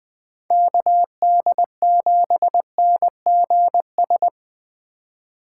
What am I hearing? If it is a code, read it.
Morse code, heard as a keyed tone, about 21 wpm, 700 Hz. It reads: KD7NGS